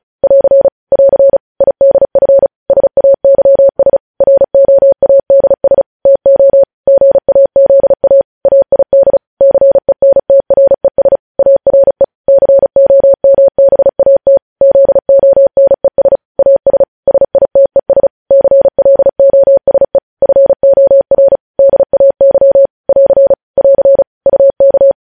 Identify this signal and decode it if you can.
Morse code, heard as a keyed tone, about 36 wpm, 550 Hz. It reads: <AR> <AR> IDF SAYS ROADS TO GAZA AID CENTRES ARE COMBAT ZONES AS SITES CLOSE FOR DAY <AR> <AR> UK
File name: podcast / News headlines-1x-35x35